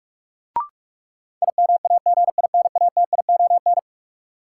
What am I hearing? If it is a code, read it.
Morse code, heard as a keyed tone, about 46 wpm, 700 Hz. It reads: IMAGINATION
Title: Sentences from Top 1300 Words - New Word Drill Alt - 45wpm